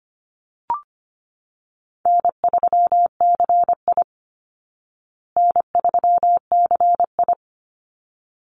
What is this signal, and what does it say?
Morse code, heard as a keyed tone, about 25 wpm, 700 Hz. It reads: N3CI N3CI
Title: Easy Call Signs Repeated 2x - 25wpm